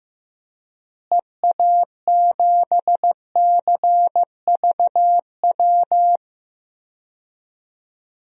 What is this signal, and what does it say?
Morse code, heard as a keyed tone, about 15 wpm, 700 Hz. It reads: EA7CVW
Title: DXCC-100 Call Signs - 6 Characters - Repeated 3x - 15wpm